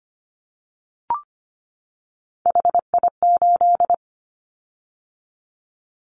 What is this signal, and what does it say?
Morse code, heard as a keyed tone, about 25 wpm, 700 Hz. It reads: HI8
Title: Callsign Prefixes for Top 100 DXCC - 25wpm